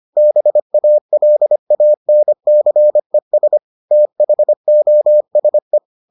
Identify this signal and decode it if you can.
Morse code, heard as a keyed tone, about 25 wpm, 600 Hz. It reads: BALANCES THOSE